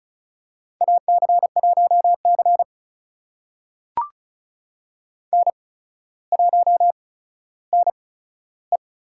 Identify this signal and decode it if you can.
Morse code, heard as a keyed tone, about 34 wpm, spaced as for 15 wpm, 700 Hz. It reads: AC1C N1NE
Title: USA Call Signs - 4 Characters - 8x Character Spacing - 35wpm